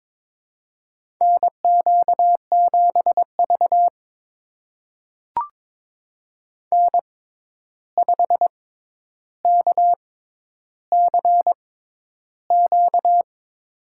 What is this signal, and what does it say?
Morse code, heard as a keyed tone, about 22 wpm, spaced as for 11 wpm, 700 Hz. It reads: NQ7V N5KCQ
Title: USA Call Signs - 6x Character Spacing - 22wpm